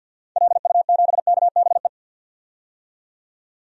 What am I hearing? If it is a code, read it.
Morse code, heard as a keyed tone, about 49 wpm, 700 Hz. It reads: LU7KBE
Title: DXCC-100 Call Signs - 6 Characters - 50wpm